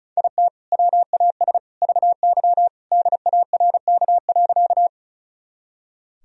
Morse code, heard as a keyed tone, about 36 wpm, 700 Hz. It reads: IT WAS VY DARK.
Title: Sentences from Top 500 Words and Abbreviations 35wpm